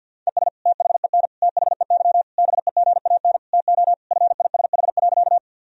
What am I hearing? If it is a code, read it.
Morse code, heard as a keyed tone, about 52 wpm, 700 Hz. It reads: ES THEN THEY BEGAN TO FISH.
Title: Sentences from Top 300 Words and Abbreviations 50wpm